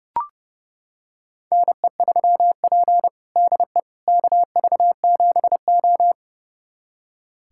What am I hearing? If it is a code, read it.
Morse code, heard as a keyed tone, about 30 wpm, 700 Hz. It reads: NE3P DE KV7O
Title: USA Callsign DE Callsign - 4 Characters - 30wpm